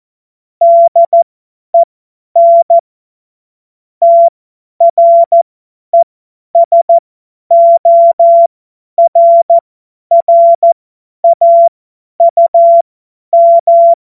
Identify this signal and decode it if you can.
Morse code, heard as a keyed tone, about 14 wpm, spaced as for 10 wpm, 680 Hz. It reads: DEN TRESORRAUM
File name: Morsekurs / Witze / Witz 13 Mai 2025 50 F70